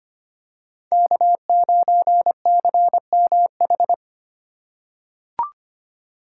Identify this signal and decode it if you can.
Morse code, heard as a keyed tone, about 25 wpm, 700 Hz. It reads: K9CMH
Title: USA Call Signs - 5 Characters - 3x Character Spacing - 25wpm